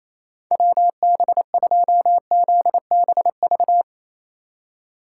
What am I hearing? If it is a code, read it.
Morse code, heard as a keyed tone, about 27 wpm, 700 Hz. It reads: WB2ZBV
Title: USA Call Signs - 6x Character Spacing - 28wpm